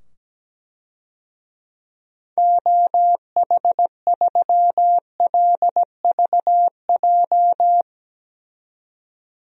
Morse code, heard as a keyed tone, about 17 wpm, 700 Hz. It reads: OH3LVJ
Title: DXCC-100 Call Signs - 6 Characters - 17wpm